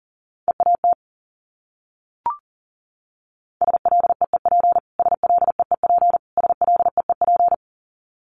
Morse code, heard as a keyed tone, about 40 wpm, 700 Hz. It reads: EAT SLEEP SLEEP SLEEP
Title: Top 500 Words Repeated 40wpm